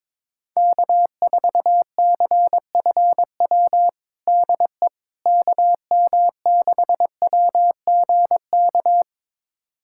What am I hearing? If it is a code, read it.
Morse code, heard as a keyed tone, about 21 wpm, 700 Hz. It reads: K4CFW DE KM6WGK
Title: USA Callsign DE Callsign 22wpm